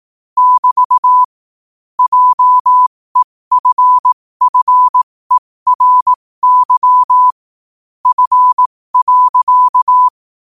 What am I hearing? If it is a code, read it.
Morse code, heard as a keyed tone, about 18 wpm, spaced as for 15 wpm, 1000 Hz. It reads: <BT> JEFFERY F.
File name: qotdfiles / 15wpm